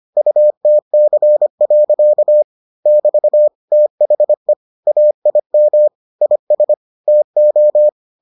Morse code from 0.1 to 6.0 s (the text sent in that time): UTC. <BT> THE AIM